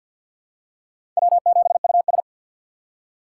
Morse code, heard as a keyed tone, about 52 wpm, 700 Hz. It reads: W7US
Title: USA Call Signs - 4 Characters - 2x Character Spacing - 50wpm